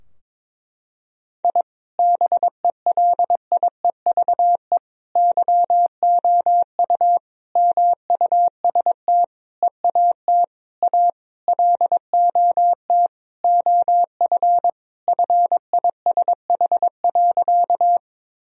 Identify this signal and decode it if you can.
Morse code, heard as a keyed tone, about 22 wpm, 700 Hz. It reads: I BELIEVE YOU MUST EAT A LOT OF FISH.
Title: Sentences from Top 1000 Words - Review Part 3 - 22wpm